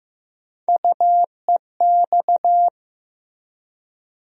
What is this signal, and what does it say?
Morse code, heard as a keyed tone, about 15 wpm, 700 Hz. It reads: UEX
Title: USA Call Sign Postfixes - 3 Letters 15wpm